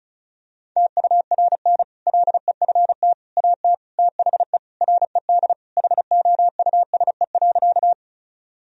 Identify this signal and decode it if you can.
Morse code, heard as a keyed tone, about 36 wpm, 700 Hz. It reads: TURN LEFT AT THE RED HOUSE.
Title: Sentences from Top 1000 Words - Review Part 2 - 35wpm